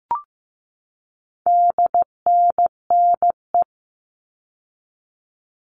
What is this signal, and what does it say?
Morse code, heard as a keyed tone, about 15 wpm, 700 Hz. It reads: DNNE
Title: All Cut Numbers - 4-Digit 15wpm